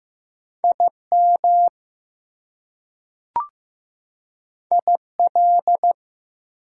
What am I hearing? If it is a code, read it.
Morse code, heard as a keyed tone, about 15 wpm, 700 Hz. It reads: IM IL
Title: Prefixes and Suffixes 15wpm